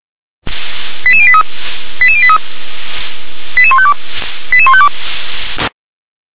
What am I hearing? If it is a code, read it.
Five-tone selective call (ZVEI), heard as quick runs of five tones, tones of about 70 ms. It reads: 88083 88083 88152 88152